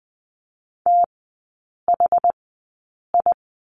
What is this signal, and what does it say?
Morse code, heard as a keyed tone, about 20 wpm, spaced as for 8 wpm, 700 Hz. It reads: THI
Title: 060 Lesson 11e - Letter h - Three Characters 20wpm